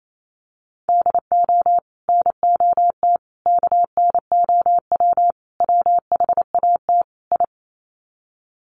Morse code, heard as a keyed tone, about 28 wpm, 700 Hz. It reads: DO NOT KNOW WHAT I
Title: Sets of 5 Words 28wpm